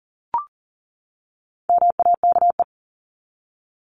Morse code, heard as a keyed tone, about 40 wpm, 700 Hz. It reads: MAKE
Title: Top 500 Words 40wpm